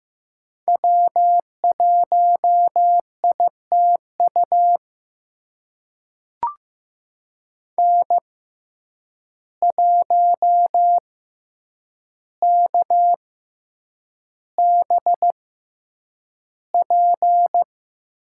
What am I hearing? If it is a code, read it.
Morse code, heard as a keyed tone, about 15 wpm, spaced as for 8 wpm, 700 Hz. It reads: W1ITU N1KBP
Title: USA Call Signs - 5 Characters - 6x Character Spacing - 15wpm